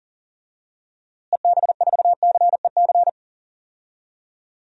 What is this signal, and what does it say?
Morse code, heard as a keyed tone, about 40 wpm, 700 Hz. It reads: EB4CEC
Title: DXCC-100 Call Signs - 6 Characters - Repeated 3x - 40wpm